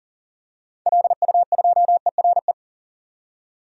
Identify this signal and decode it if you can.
Morse code, heard as a keyed tone, about 40 wpm, 700 Hz. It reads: LU2ERE
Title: DXCC-100 Call Signs - 6 Characters - 40wpm